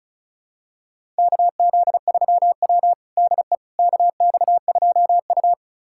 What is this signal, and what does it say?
Morse code, heard as a keyed tone, about 36 wpm, 700 Hz. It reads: KZ3W DE KX2U
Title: USA Callsign DE Callsign - 4 Characters - 35wpm